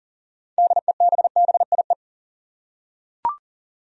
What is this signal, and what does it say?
Morse code, heard as a keyed tone, about 38 wpm, 700 Hz. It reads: DEBBIE
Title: USA Common Names - 40wpm